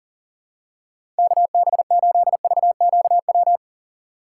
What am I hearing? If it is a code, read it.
Morse code, heard as a keyed tone, about 40 wpm, 700 Hz. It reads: KB8VQW
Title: USA Call Signs - 8x Character Spacing - 40wpm